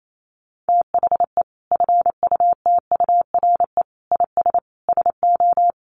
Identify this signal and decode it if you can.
Morse code, heard as a keyed tone, about 28 wpm, 700 Hz. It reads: THE FUTURE IS SO
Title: Sentences from Top 700 Words 28wpm